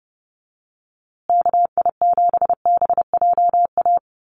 Morse code, heard as a keyed tone, about 30 wpm, 700 Hz. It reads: KI7BJA
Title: USA Call Signs Repeated 3x 30wpm